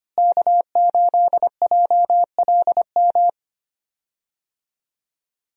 Morse code, heard as a keyed tone, about 25 wpm, 700 Hz. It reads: K8JLM
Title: USA Call Signs - 5 Characters - Repeated 2x - 25wpm